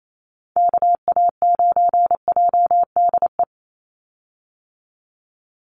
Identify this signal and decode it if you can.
Morse code, heard as a keyed tone, about 28 wpm, 700 Hz. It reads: KA9JDE
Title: USA Call Signs 28wpm